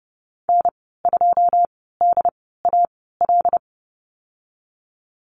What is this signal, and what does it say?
Morse code, heard as a keyed tone, about 30 wpm, spaced as for 17 wpm, 700 Hz. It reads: N2DAL
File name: Lesson 14g - Number 2 - Call Signs 30wpm